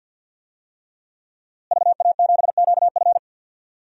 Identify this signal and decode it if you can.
Morse code, heard as a keyed tone, about 46 wpm, 700 Hz. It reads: VA7QF
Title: DXCC-100 Call Signs - 5 Characters - Repeated 3x - 50wpm